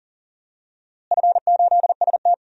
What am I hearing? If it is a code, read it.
Morse code, heard as a keyed tone, about 40 wpm, 700 Hz. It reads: F8ST
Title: DXCC-100 Call Signs - 4 Characters - Repeated 3x - 40wpm